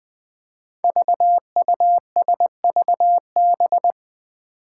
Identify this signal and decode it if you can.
Morse code, heard as a keyed tone, about 20 wpm, 700 Hz. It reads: VUSVB
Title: Five Challenging Letter-Number Combinations 20wpm